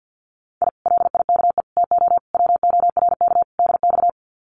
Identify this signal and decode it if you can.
Morse code, heard as a keyed tone, about 50 wpm, 700 Hz. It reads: I LIKE TO WORK DX
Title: Common QSO Phrases Encore 50wpm